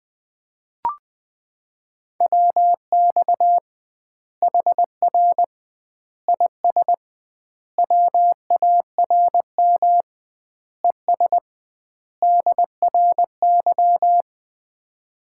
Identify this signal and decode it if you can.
Morse code, heard as a keyed tone, about 20 wpm, 700 Hz. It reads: WX HR IS WARM ES DRY
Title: Common QSO Phrases - 2x Word-Spacing - 20wpm